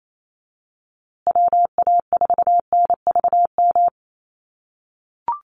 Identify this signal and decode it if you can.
Morse code, heard as a keyed tone, about 28 wpm, 700 Hz. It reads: WA4NVM
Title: USA Call Signs Repeated 2x 28wpm